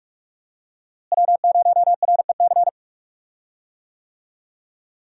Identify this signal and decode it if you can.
Morse code, heard as a keyed tone, about 46 wpm, 700 Hz. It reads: W0REC
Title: USA Call Signs - 5 Characters - Repeated 3x - 45wpm